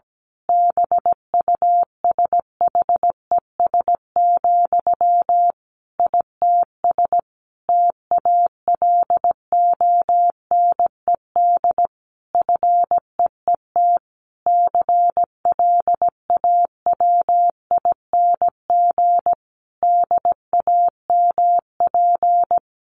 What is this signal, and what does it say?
Morse code, heard as a keyed tone, about 17 wpm, 700 Hz. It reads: BUSHES, ITS TALONED FEET CLAWING DAMP
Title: A Sound Of Thunder 17wpm